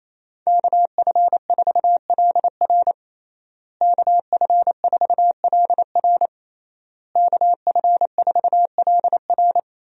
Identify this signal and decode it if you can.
Morse code, heard as a keyed tone, about 28 wpm, 700 Hz. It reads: KF4LR KF4LR KF4LR
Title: USA Call Signs - 5 Characters - Repeated 3x - 28wpm